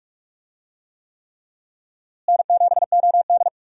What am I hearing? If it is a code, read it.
Morse code, heard as a keyed tone, about 43 wpm, 700 Hz. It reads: N7OD
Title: USA Call Signs - 4 Characters - Repeated 3x - 45wpm